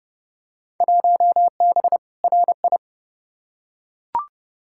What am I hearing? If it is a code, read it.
Morse code, heard as a keyed tone, about 31 wpm, 700 Hz. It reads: 1B RI
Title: Field Day Exchanges Encore - 30wpm